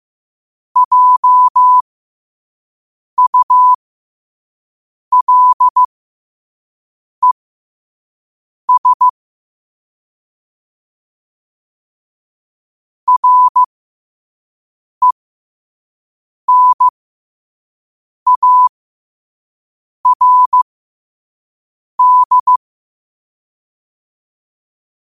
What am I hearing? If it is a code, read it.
Morse code, heard as a keyed tone, about 15 wpm, spaced as for 5 wpm, 1000 Hz. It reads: JULES RENARD
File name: qotdfiles / 5wpm